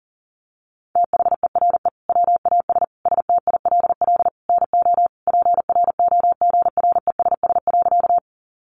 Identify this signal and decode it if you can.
Morse code, heard as a keyed tone, about 40 wpm, 700 Hz. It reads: THERE WAS STILL NO PROGRESS.
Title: Sentences from Top 1400 Words 40wpm